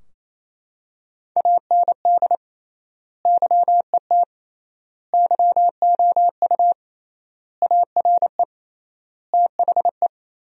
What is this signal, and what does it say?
Morse code, heard as a keyed tone, about 28 wpm, 700 Hz. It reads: AND YET YOU ARE THE
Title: 3x Word-Spacing - Sentences from Top 500 Words - 28wpm